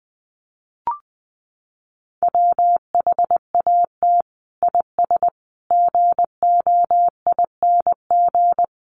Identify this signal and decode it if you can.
Morse code, heard as a keyed tone, about 20 wpm, 700 Hz. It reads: WHAT IS GOING
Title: Sets of 5 Words Encore2 20wpm